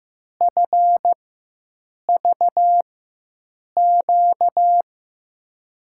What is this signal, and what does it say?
Morse code, heard as a keyed tone, about 15 wpm, spaced as for 7 wpm, 700 Hz. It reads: FVQ
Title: USA Call Signs - 6 Characters - 4x Character Spacing - 15wpm